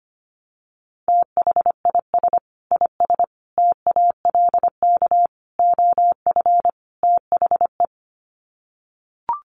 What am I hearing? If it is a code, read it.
Morse code, heard as a keyed tone, about 25 wpm, 700 Hz. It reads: THIS IS TALK OF THE
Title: Sets of 5 Words 25wpm